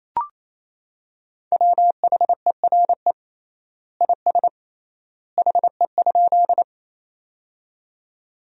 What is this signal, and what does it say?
Morse code, heard as a keyed tone, about 28 wpm, 700 Hz. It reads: WHERE IS HE?
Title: 3x Word-Spacing - Sentences from Top 200 Words - 28wpm